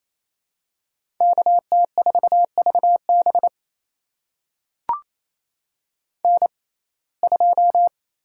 Morse code, heard as a keyed tone, about 28 wpm, spaced as for 20 wpm, 700 Hz. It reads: KT4VB N2
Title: USA Call Signs - 5 Characters - 6x Character Spacing - 28wpm